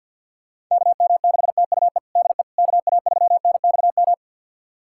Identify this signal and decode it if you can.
Morse code, heard as a keyed tone, about 51 wpm, 700 Hz. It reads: KM6TUE DE KA3NXM
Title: USA Callsign DE Callsign 50wpm